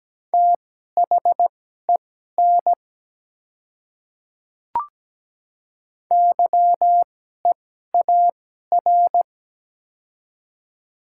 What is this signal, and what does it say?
Morse code, heard as a keyed tone, about 17 wpm, spaced as for 12 wpm, 700 Hz. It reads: THEN YEAR
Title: 2x Character Spacing - Top 300 Words - 17wpm